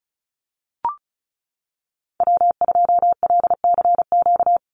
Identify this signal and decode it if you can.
Morse code, heard as a keyed tone, about 34 wpm, 700 Hz. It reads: W2LCQ
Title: Call Signs Repeated 3x - 35wpm